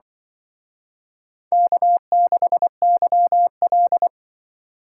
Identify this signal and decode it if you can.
Morse code, heard as a keyed tone, about 25 wpm, 700 Hz. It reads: K6YL